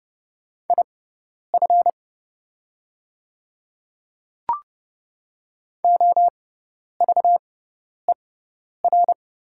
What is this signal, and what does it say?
Morse code, heard as a keyed tone, about 30 wpm, spaced as for 10 wpm, 700 Hz. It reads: IF OVER
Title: 6x Character Spacing - Top 200 Words - 30wpm